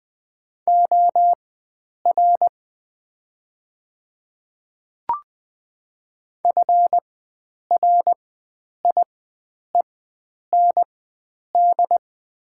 Morse code, heard as a keyed tone, about 20 wpm, spaced as for 9 wpm, 700 Hz. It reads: OR FRIEND
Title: 4x Character Spacing - Top 300 Words - 20wpm